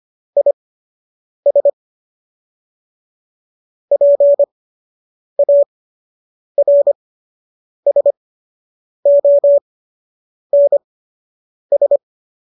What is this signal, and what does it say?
Morse code, heard as a keyed tone, about 25 wpm, spaced as for 8 wpm, 570 Hz. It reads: IS PARSONS
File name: qso003 08